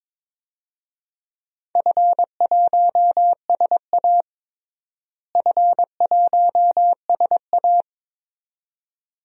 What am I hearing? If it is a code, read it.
Morse code, heard as a keyed tone, about 22 wpm, 700 Hz. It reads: F1SA F1SA
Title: DXCC-100 Call Signs - 4 Characters - Repeated 2x - 22wpm